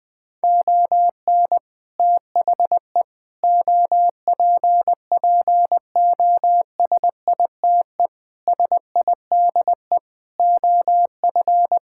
Morse code, heard as a keyed tone, about 20 wpm, 700 Hz. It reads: ON THE OPPOSITE SIDE OF
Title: Sets of 6 Words - 20wpm